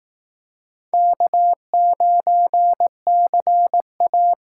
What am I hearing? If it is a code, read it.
Morse code, heard as a keyed tone, about 19 wpm, 700 Hz. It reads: K9CA
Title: Speed Racing - USA Call Signs 15wpm